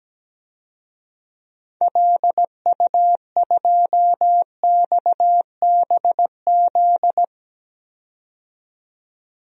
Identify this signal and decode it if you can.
Morse code, heard as a keyed tone, about 17 wpm, 700 Hz. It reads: LU2XBZ
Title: DXCC-100 Call Signs - 6 Characters - 17wpm